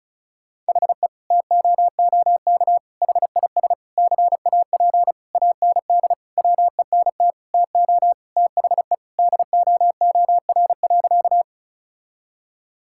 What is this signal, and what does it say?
Morse code, heard as a keyed tone, about 34 wpm, 700 Hz. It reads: HE TOOK HIS CAP AND WENT TO THE DOOR.
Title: Sentences from Top 1600 Words 35wpm